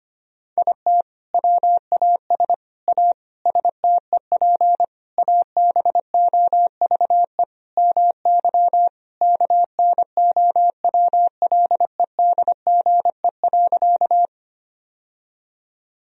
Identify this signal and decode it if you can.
Morse code, heard as a keyed tone, about 25 wpm, 700 Hz. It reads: IT WAS A STEP ABOVE MY KNOWLEDGE.
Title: Sentences from Top 1000 Words - Review Part 3 - 25wpm